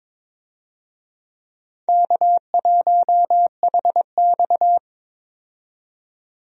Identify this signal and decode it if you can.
Morse code, heard as a keyed tone, about 22 wpm, 700 Hz. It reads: K1HX
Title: USA Call Signs - 4 Characters - 22wpm